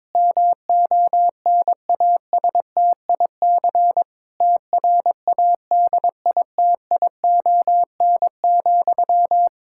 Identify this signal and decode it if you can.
Morse code, heard as a keyed tone, about 22 wpm, 700 Hz. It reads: MONASTIC TRADITION,
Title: Pangram Challenge v2 - 22wpm